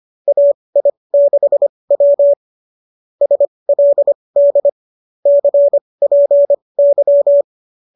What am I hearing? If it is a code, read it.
Morse code, heard as a keyed tone, about 25 wpm, spaced as for 20 wpm, 590 Hz. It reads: AI6W SLD CPY